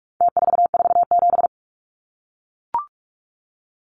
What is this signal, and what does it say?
Morse code, heard as a keyed tone, about 44 wpm, 700 Hz. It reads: T447
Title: Numbers - Contest Style - 4-Digit 45wpm